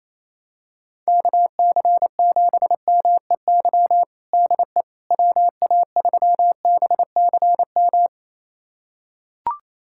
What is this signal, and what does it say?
Morse code, heard as a keyed tone, about 28 wpm, 700 Hz. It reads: KC7MEY DE WA3BCM
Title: USA Callsign DE Callsign 28wpm